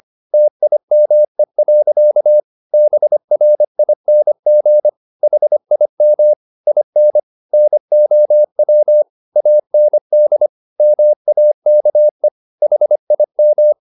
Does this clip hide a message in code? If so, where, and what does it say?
Yes: Morse code from 0.3 to 13.7 s, heard as a keyed tone, about 25 wpm, 600 Hz. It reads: TIME. BRING HIM IN NOW AND MAKE HIM